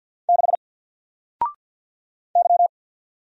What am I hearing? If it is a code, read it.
Morse code, heard as a keyed tone, about 48 wpm, 700 Hz. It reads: 6 Y
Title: Single Character (Letter:Number:Punctuation:Prosign) 50wpm